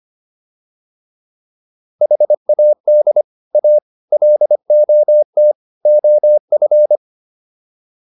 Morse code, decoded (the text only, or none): HAD A LOT OF